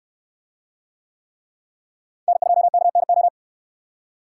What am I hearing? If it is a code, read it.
Morse code, heard as a keyed tone, about 62 wpm, 700 Hz. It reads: N2KNO